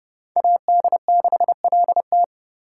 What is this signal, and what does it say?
Morse code, heard as a keyed tone, about 28 wpm, 700 Hz. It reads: AD6LT